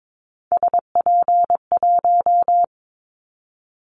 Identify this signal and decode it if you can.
Morse code, heard as a keyed tone, about 22 wpm, 700 Hz. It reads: SP1
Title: Callsign Prefixes for Top 25 DXCC - 22wpm